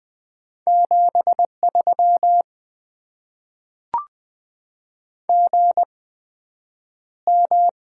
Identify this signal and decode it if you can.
Morse code, heard as a keyed tone, about 20 wpm, spaced as for 9 wpm, 700 Hz. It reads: 73 GM
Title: Top 20 QSO Elements - 8x Character Spacing - 20wpm